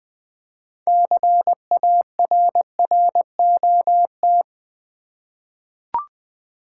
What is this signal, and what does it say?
Morse code, heard as a keyed tone, about 20 wpm, 700 Hz. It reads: CARROT
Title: 6 Letter Words - First 3 Letters Given - 20wpm